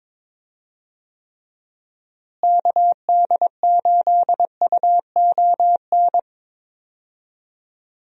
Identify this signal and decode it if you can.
Morse code, heard as a keyed tone, about 22 wpm, 700 Hz. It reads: KD8UON
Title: USA Call Signs - 6 Characters - Repeated 2x - 22wpm